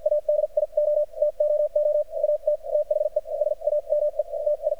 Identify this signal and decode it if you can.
Morse code, heard as a keyed tone, about 52 wpm, spaced as for 42 wpm, 600 Hz. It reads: ADIO TOO AT THE SAME TI
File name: archived / CRSnet